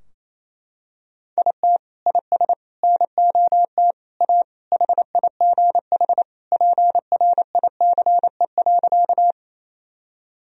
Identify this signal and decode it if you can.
Morse code, heard as a keyed tone, about 27 wpm, 700 Hz. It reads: IT IS NOT A HIGH PRICE.
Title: Sentences from Top 1000 Words - Review Part 4 - 28wpm